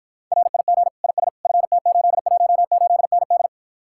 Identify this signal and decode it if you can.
Morse code, heard as a keyed tone, about 53 wpm, 700 Hz. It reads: RIG IS FT818ND